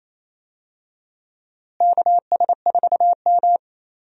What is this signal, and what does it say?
Morse code, heard as a keyed tone, about 27 wpm, 700 Hz. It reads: KS4M
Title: USA Call Signs - 4 Characters - Repeated 3x - 28wpm